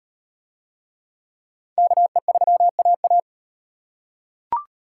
Morse code, heard as a keyed tone, about 38 wpm, 700 Hz. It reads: KE3AA